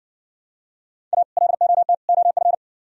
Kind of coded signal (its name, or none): Morse code